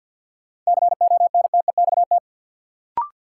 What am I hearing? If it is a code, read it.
Morse code, heard as a keyed tone, about 53 wpm, 700 Hz. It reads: CONTEXT
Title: Sentences from Top 1000 Words - Review Part 7 - New Word Drill - 50wpm